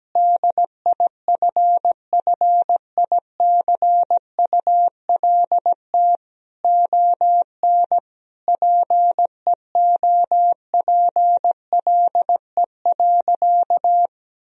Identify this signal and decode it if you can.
Morse code, heard as a keyed tone, about 17 wpm, 700 Hz. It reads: DIFFICULT ON PEOPLE.
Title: Sentences from Top 1000 Words - Review Part 3 - 17wpm